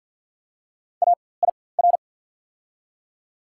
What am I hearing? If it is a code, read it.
Morse code, heard as a keyed tone, about 54 wpm, spaced as for 24 wpm, 690 Hz. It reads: AIR
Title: 4x Character Spacing - Top 300 Words - 50wpm